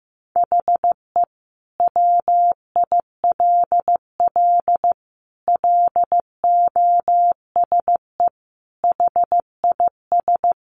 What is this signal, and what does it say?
Morse code, heard as a keyed tone, about 15 wpm, 700 Hz. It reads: HE WILL LOSE HIS
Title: Sentences from Top 1100 Words 15wpm